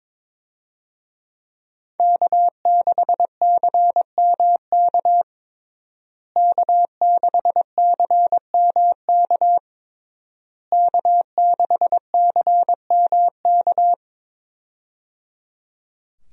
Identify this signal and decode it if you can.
Morse code, heard as a keyed tone, about 22 wpm, 700 Hz. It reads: K6CMK K6CMK K6CMK
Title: USA Call Signs - 5 Characters - Repeated 3x - 22wpm